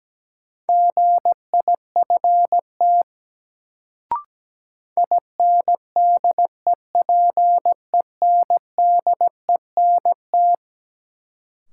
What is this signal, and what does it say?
Morse code, heard as a keyed tone, about 17 wpm, 700 Hz. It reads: GIFT INDEPENDENT
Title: Sentences from Top 900 Words - New Word Drill - 17wpm